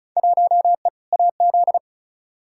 Morse code, heard as a keyed tone, about 34 wpm, 700 Hz. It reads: 1E AZ
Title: Field Day Exchanges - 35wpm